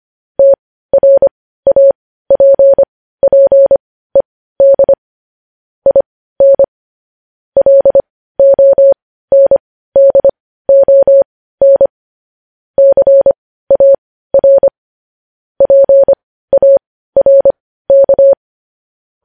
Morse code, heard as a keyed tone, about 25 wpm, spaced as for 15 wpm, 550 Hz. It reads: TRAPPED IN LONDON CAR PARK
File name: podcast / News headlines-1x-25x15